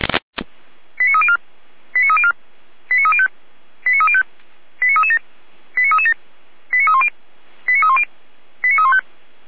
Five-tone selective call (ZVEI), heard as quick runs of five tones, tones of about 70 ms. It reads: 89304 89304 89305 89305 89337 89337 89310 89310 89315